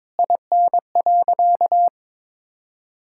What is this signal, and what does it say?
Morse code, heard as a keyed tone, about 22 wpm, 700 Hz. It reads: IN.
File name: practice / 22wpm / Sentences with 2-Word Phrases - 22wpm